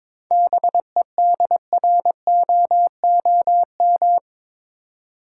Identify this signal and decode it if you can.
Morse code, heard as a keyed tone, about 22 wpm, 700 Hz. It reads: BEDROOM
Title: Seven Letter Words - 22wpm